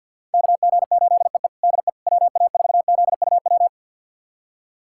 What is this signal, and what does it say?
Morse code, heard as a keyed tone, about 49 wpm, 700 Hz. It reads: KG8EE DE WA4ZUW
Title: USA Callsign DE Callsign 50wpm